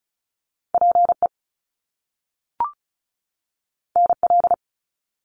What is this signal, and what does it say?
Morse code, heard as a keyed tone, about 36 wpm, 700 Hz. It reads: PE NL
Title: Canadian Provinces and Territories 35wpm